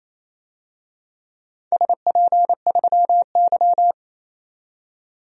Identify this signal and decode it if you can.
Morse code, heard as a keyed tone, about 27 wpm, 700 Hz. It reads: SP3Y
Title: DXCC-100 Call Signs - 4 Characters - 28wpm